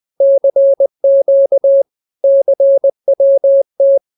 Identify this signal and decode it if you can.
Morse code, heard as a keyed tone, about 20 wpm, 550 Hz. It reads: CQ CWT